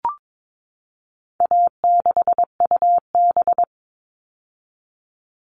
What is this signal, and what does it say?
Morse code, heard as a keyed tone, about 22 wpm, 700 Hz. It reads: A6UB
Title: All Cut Numbers - 4-Digit 22wpm